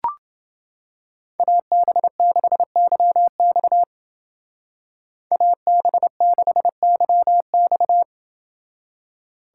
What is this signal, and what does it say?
Morse code, heard as a keyed tone, about 30 wpm, 700 Hz. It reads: AB6YX AB6YX